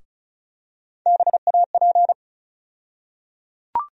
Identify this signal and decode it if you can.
Morse code, heard as a keyed tone, about 36 wpm, 700 Hz. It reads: BAP
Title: USA Call Sign Postfixes - 3 Letters 35wpm